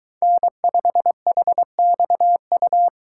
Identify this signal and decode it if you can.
Morse code, heard as a keyed tone, about 21 wpm, 700 Hz. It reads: N5HXU